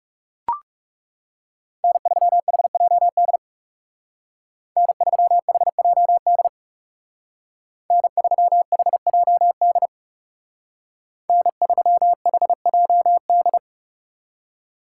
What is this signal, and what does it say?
Morse code, heard as a keyed tone, about 37 wpm, 700 Hz. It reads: N3HJD N3HJD N3HJD N3HJD
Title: Speed Racing - USA 1x3 Call Signs 30wpm